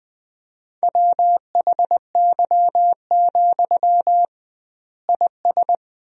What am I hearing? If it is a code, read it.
Morse code, heard as a keyed tone, about 21 wpm, 700 Hz. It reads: WHY, IS
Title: 2x Word-Spacing - Sentences from Top 300 Words - 20wpm